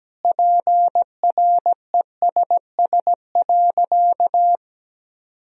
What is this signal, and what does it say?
Morse code, heard as a keyed tone, about 17 wpm, 700 Hz. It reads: PRESS.
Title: Sentences from Top 1000 Words - Review Part 1 - 17wpm